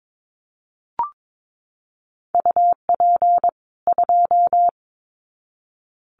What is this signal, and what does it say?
Morse code, heard as a keyed tone, about 22 wpm, 700 Hz. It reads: UP 2
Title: Common QSO Phrases 22wpm